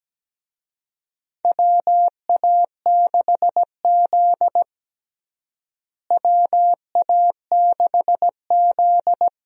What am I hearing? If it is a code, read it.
Morse code, heard as a keyed tone, about 17 wpm, 700 Hz. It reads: WA6Z WA6Z
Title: USA Call Signs - 4 Characters - Repeated 3x - 17wpm